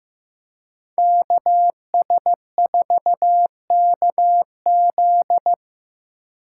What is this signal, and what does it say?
Morse code, heard as a keyed tone, about 15 wpm, 700 Hz. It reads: KS4KZ
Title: USA Call Signs - 4x Character Spacing - 15wpm